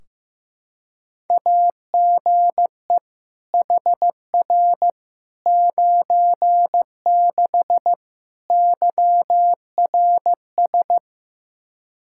Morse code, heard as a keyed tone, about 15 wpm, 700 Hz. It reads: AGE HR 96 YRS
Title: Practice QSO - Exchange 3 - More Info - 15wpm